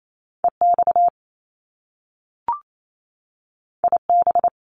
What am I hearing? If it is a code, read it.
Morse code, heard as a keyed tone, about 28 wpm, 700 Hz. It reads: EX IB
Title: Top Two Letter Combinations in Words 28wpm